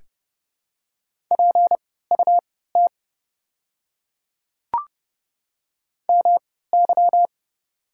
Morse code, heard as a keyed tone, about 31 wpm, spaced as for 17 wpm, 700 Hz. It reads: PUT MY
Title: 3x Character Spacing - Top 200 Words - 30wpm